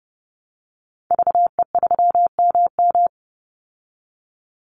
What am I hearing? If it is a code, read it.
Morse code, heard as a keyed tone, about 30 wpm, 700 Hz. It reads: VE3MM
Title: Call Signs Repeated 2x - 30wpm